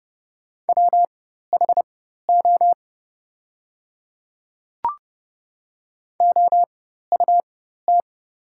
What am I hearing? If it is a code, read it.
Morse code, heard as a keyed tone, about 29 wpm, spaced as for 14 wpm, 700 Hz. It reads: WHO OUT
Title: 4x Character Spacing - Top 100 Words - 30wpm